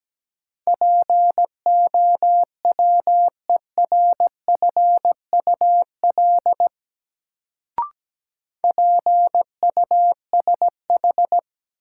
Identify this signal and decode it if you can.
Morse code, heard as a keyed tone, about 17 wpm, 700 Hz. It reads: POWERFUL PUSH
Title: Sentences from Top 1300 Words - New Word Drill Alt - 17wpm